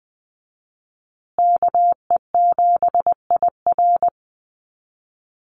Morse code, heard as a keyed tone, about 20 wpm, 700 Hz. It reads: KE7IR
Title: USA Call Signs Repeated 2x 20wpm